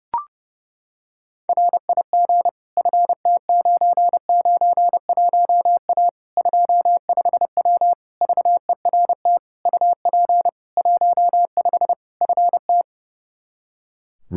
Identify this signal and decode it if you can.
Morse code, heard as a keyed tone, about 32 wpm, 700 Hz. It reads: RIG FT991A 25W VERT UP 15 FT
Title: Practice QSO - Exchange 2 - Info - 30wpm